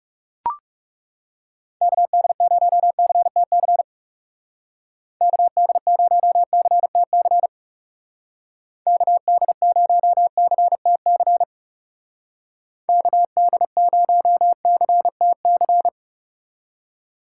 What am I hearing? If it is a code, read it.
Morse code, heard as a keyed tone, about 35 wpm, 700 Hz. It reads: KD0CTC KD0CTC KD0CTC KD0CTC
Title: Speed Racing - USA 2x3 Call Signs 30wpm